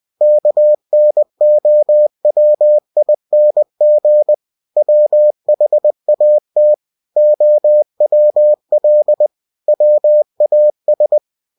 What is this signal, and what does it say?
Morse code, heard as a keyed tone, about 20 wpm, 600 Hz. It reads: KNOWING WHAT OWL WAS
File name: cw-books / winnie-the-pooh / winnie-20wpm-page-0031